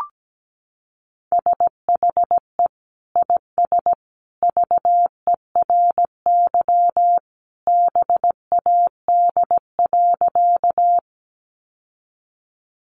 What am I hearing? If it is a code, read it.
Morse code, heard as a keyed tone, about 17 wpm, 700 Hz. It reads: SHE IS VERY BAD.
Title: Sentences from Top 700 Words 17wpm